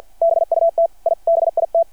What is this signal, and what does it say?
Morse code, heard as a keyed tone, about 51 wpm, 660 Hz. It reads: BUT I BIT
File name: archived / CRSnet